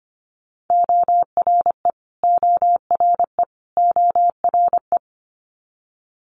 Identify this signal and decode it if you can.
Morse code, heard as a keyed tone, about 25 wpm, 700 Hz. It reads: ORE ORE ORE
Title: Multi-letter Phonemes Repeated 3x 25wpm